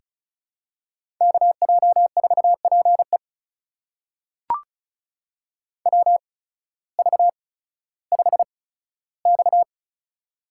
Morse code, heard as a keyed tone, about 36 wpm, spaced as for 16 wpm, 700 Hz. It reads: KJ4PE WV5X
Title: USA Call Signs - 8x Character Spacing - 35wpm